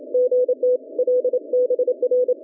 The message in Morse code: GT LBR